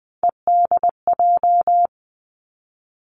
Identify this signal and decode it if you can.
Morse code, heard as a keyed tone, about 20 wpm, 700 Hz. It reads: EDJ
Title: Three Challenging Letter-Number Combinations 20wpm